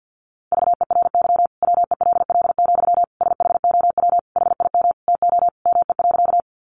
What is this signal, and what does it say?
Morse code, heard as a keyed tone, about 50 wpm, 700 Hz. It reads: VERY WELL, SHOW HIM TO ME.
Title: Sentences from Top 200 Words 50wpm